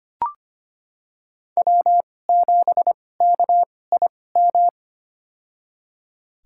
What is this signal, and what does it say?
Morse code, heard as a keyed tone, about 25 wpm, spaced as for 18 wpm, 700 Hz. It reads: W7KIM
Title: USA Call Signs - 2x Character Spacing - 25wpm